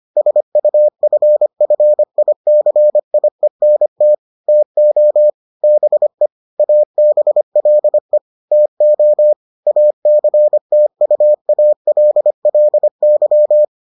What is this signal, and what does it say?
Morse code, heard as a keyed tone, about 25 wpm, 600 Hz. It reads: SUFFICIENT TO BE ABLE TO ACTUALLY